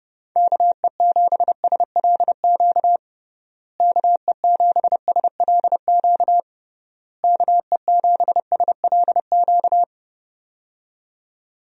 Morse code, heard as a keyed tone, about 30 wpm, 700 Hz. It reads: KE7SLQ KE7SLQ KE7SLQ
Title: USA Call Signs - 6 Characters - Repeated 3x - 30wpm